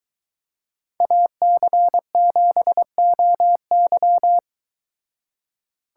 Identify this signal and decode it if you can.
Morse code, heard as a keyed tone, about 22 wpm, 700 Hz. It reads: AC7OY